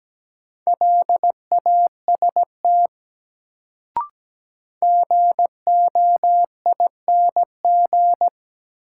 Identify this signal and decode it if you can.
Morse code, heard as a keyed tone, about 17 wpm, 700 Hz. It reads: LAST GOING
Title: Sets of 2 Words - New Word Drill - 17wpm